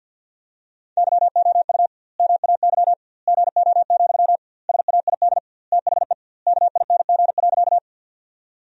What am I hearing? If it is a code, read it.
Morse code, heard as a keyed tone, about 50 wpm, 700 Hz. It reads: YOU MAY GO, SAID THE KING.
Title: 2x Word-Spacing - Sentences from Top 400 Words - 50wpm